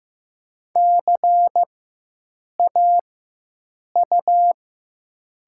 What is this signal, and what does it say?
Morse code, heard as a keyed tone, about 15 wpm, spaced as for 7 wpm, 700 Hz. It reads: CAU
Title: 4x Character Spacing - Top 200 Words - 15wpm